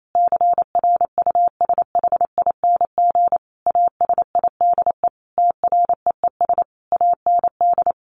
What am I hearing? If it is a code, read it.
Morse code, heard as a keyed tone, about 28 wpm, 700 Hz. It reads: CRUSHING ASIDE TREES AND
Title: A Sound Of Thunder 28wpm